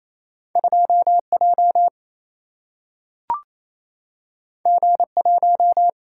Morse code, heard as a keyed tone, about 28 wpm, 700 Hz. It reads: 2 J G 1
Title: Single-Letter Single-Number 28wpm